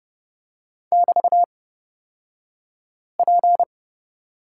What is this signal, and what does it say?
Morse code, heard as a keyed tone, about 30 wpm, spaced as for 5 wpm, 700 Hz. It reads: <BT>P
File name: Lesson 37d - Letter x - Two Characters 30wpm